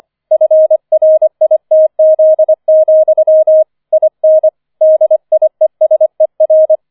Morse code, heard as a keyed tone, about 25 wpm, 630 Hz. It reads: FRITZ, IN DIESER
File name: 005 Fünfter Streich 25wpm